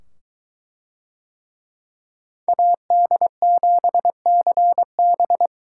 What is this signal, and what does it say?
Morse code, heard as a keyed tone, about 22 wpm, 700 Hz. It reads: AD7CB